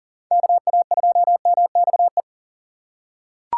Morse code, heard as a keyed tone, about 40 wpm, 700 Hz. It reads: KA2MXE
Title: USA Call Signs - 6x Character Spacing - 40wpm